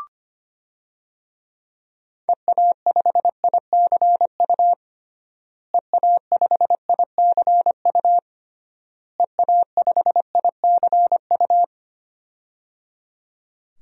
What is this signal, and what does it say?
Morse code, heard as a keyed tone, about 25 wpm, 700 Hz. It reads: EA5ICU EA5ICU EA5ICU
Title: DXCC-100 Call Signs - 6 Characters - Repeated 3x - 25wpm